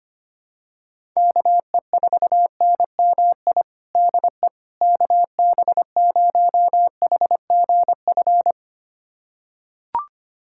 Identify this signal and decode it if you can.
Morse code, heard as a keyed tone, about 25 wpm, 700 Hz. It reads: KE4NMI DE KB0HGF
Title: USA Callsign DE Callsign - 6 Characters - 25wpm